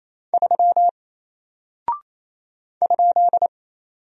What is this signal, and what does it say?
Morse code, heard as a keyed tone, about 28 wpm, 700 Hz. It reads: 3 ?
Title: Single Character (Letter:Number:Punctuation:Prosign) Alt 28wpm